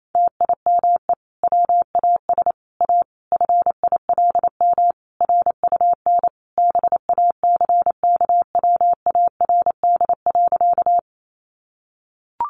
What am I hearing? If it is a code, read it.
Morse code, heard as a keyed tone, about 28 wpm, 700 Hz. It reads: TIME WAS A FILM RUN BACKWARD.
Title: A Sound Of Thunder 28wpm